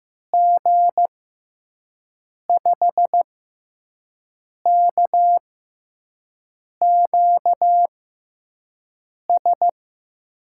Morse code, heard as a keyed tone, about 15 wpm, spaced as for 5 wpm, 700 Hz. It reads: G5KQS
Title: USA Call Signs - 6 Characters - 6x Character Spacing - 15wpm